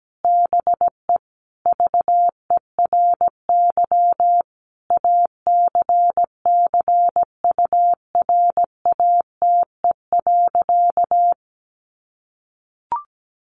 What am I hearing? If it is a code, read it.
Morse code, heard as a keyed tone, about 17 wpm, 700 Hz. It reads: BE VERY ACCURATE.
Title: Sentences from Top 1200 Words 17wpm